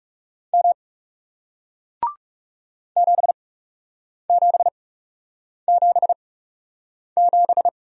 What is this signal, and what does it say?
Morse code, heard as a keyed tone, about 41 wpm, spaced as for 8 wpm, 700 Hz. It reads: M 7777